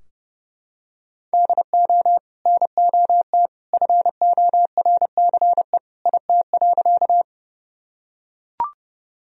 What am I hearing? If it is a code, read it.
Morse code, heard as a keyed tone, about 32 wpm, 700 Hz. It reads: DO NOT FORCE IT.
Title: Sentences from Top 1000 Words - Review Part 3 - 30wpm